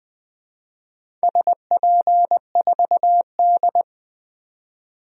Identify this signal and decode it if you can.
Morse code, heard as a keyed tone, about 20 wpm, 700 Hz. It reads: SP4D
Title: DXCC-100 Call Signs - 4 Characters - 20wpm